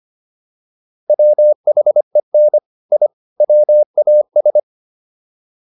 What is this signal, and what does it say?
Morse code, heard as a keyed tone, about 25 wpm, 600 Hz. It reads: WHEN I WAS
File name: WD205 25